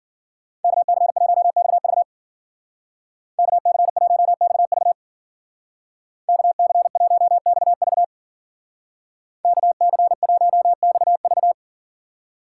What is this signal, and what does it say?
Morse code, heard as a keyed tone, about 53 wpm, 700 Hz. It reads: KC1XV KC1XV KC1XV KC1XV